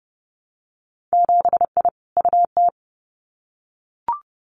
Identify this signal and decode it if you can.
Morse code, heard as a keyed tone, about 30 wpm, 700 Hz. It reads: 7I UT
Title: Winter Field Day Exchanges Encore 30wpm